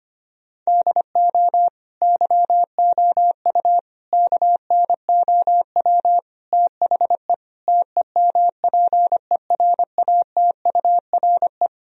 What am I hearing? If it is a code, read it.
Morse code, heard as a keyed tone, about 25 wpm, 700 Hz. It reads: DO YOU KNOW THE TEMPERATURE
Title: Sentences from Top 1000 Words - Review Part 4 - 25wpm